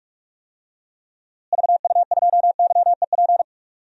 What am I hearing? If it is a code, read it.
Morse code, heard as a keyed tone, about 46 wpm, 700 Hz. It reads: VU2YEP
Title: DXCC-100 Call Signs - 6 Characters - Repeated 2x - 45wpm